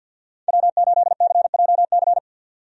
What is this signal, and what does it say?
Morse code, heard as a keyed tone, about 47 wpm, 700 Hz. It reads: W8CJC
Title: USA Call Signs - 5 Characters - 50wpm